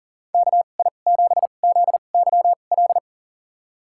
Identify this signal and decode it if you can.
Morse code, heard as a keyed tone, about 40 wpm, spaced as for 29 wpm, 700 Hz. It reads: KI7ZYL
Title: USA Call Signs - 6 Characters - 2x Character Spacing - 40wpm